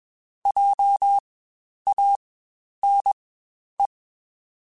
Morse code, heard as a keyed tone, about 21 wpm, spaced as for 10 wpm, 800 Hz. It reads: JANE
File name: Spaced Alphas / Spaced Alphas mp3s / Js sp @20